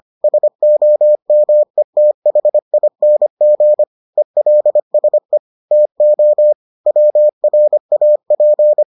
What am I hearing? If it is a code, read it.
Morse code, heard as a keyed tone, about 25 wpm, 600 Hz. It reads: SOMETHING ELSE TO WRAP